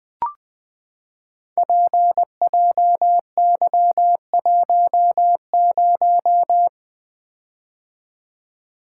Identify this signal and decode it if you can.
Morse code, heard as a keyed tone, about 20 wpm, 700 Hz. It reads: PJY10
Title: Code Groups - 5 Letter:Numbers - 20wpm